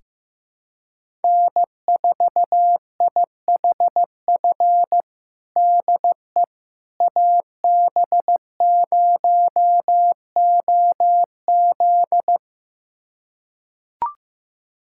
Morse code, heard as a keyed tone, about 15 wpm, 700 Hz. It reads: N4IHF DE AB0OZ
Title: USA Callsign DE Callsign - 5 Characters - 15wpm